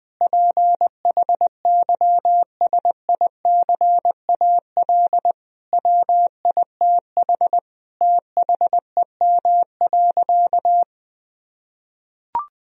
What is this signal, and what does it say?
Morse code, heard as a keyed tone, about 20 wpm, 700 Hz. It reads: PHYSICAL WITH THEM.
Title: Sentences from Top 1000 Words - Review Part 4 - 20wpm